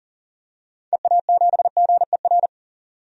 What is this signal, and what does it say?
Morse code, heard as a keyed tone, about 40 wpm, 700 Hz. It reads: EA7GER
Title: DXCC-100 Call Signs - 6 Characters - 40wpm